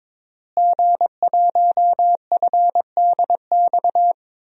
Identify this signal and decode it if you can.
Morse code, heard as a keyed tone, about 22 wpm, 700 Hz. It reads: G1FDX
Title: DXCC-100 Call Signs - 5 Characters - Repeated 3x - 22wpm